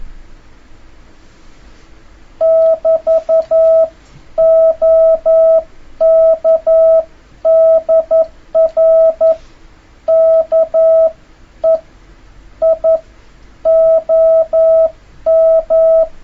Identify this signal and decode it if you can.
Morse code, heard as a keyed tone, about 11 wpm, spaced as for 9 wpm, 650 Hz. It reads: <BT> OK DR KEI OM
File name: jh8cbh / 2026cw